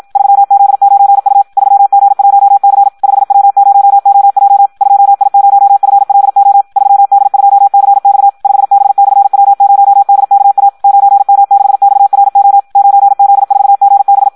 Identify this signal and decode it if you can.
Morse code, heard as a keyed tone, about 56 wpm, 800 Hz. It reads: 3Z9U 3G1C 5W0OJ 2I0FLO 3D2CX 5B/W0DKA 9M6XRO 9Z4GB
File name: Audio practica CW / qrq-tx-54 2